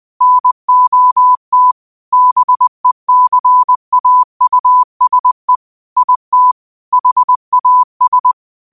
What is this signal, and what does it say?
Morse code, heard as a keyed tone, about 20 wpm, 1000 Hz. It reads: NOT BECAUSE IT HAS
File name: qotdfiles / 20wpm